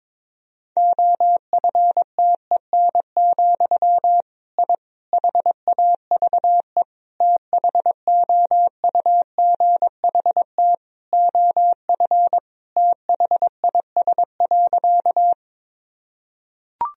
Morse code, heard as a keyed tone, about 22 wpm, 700 Hz. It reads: OFTEN, I HAVE THOUGHT OF THIS.
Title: Sentences from Top 1000 Words - Review Part 1 - 22wpm